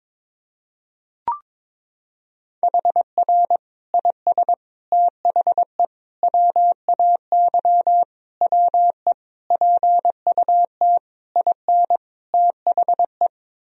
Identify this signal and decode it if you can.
Morse code, heard as a keyed tone, about 22 wpm, 700 Hz. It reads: HR IS THE WAY WE PUT IN THE
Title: Sentences from Top 200 Words and Abbreviations 22wpm